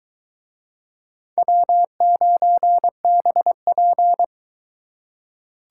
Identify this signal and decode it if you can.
Morse code, heard as a keyed tone, about 22 wpm, 700 Hz. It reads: W9BP